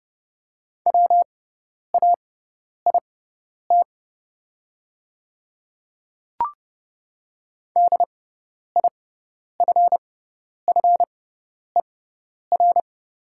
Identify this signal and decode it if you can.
Morse code, heard as a keyed tone, about 30 wpm, spaced as for 10 wpm, 690 Hz. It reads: WAIT DIFFER
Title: 6x Character Spacing - Top 500 Words - 30wpm